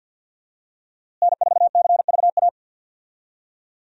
Morse code, heard as a keyed tone, about 51 wpm, 700 Hz. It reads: N4CVA
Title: USA Call Signs - 5 Characters - Repeated 3x - 50wpm